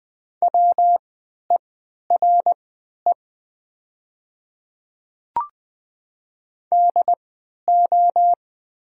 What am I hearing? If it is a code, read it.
Morse code, heard as a keyed tone, about 20 wpm, spaced as for 11 wpm, 700 Hz. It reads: WERE DO
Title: 3x Character Spacing - Top 300 Words - 20wpm